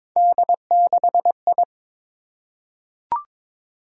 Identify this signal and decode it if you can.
Morse code, heard as a keyed tone, about 22 wpm, 700 Hz. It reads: D6I
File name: practice / 22wpm / Three Letter-Number Combinations - 22wpm